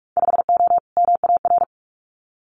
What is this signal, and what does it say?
Morse code, heard as a keyed tone, about 46 wpm, 700 Hz. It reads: 5O MAR
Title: Winter Field Day Exchanges Encore 45wpm